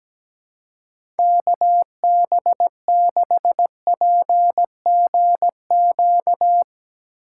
Morse code, heard as a keyed tone, about 17 wpm, 700 Hz. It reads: KB6PGQ
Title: USA Call Signs - 6 Characters - 4x Character Spacing - 17wpm